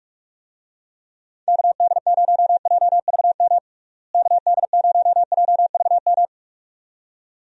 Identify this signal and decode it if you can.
Morse code, heard as a keyed tone, about 41 wpm, 700 Hz. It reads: KD0JVM KD0JVM
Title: USA Call Signs - 6 Characters - Repeated 2x - 45wpm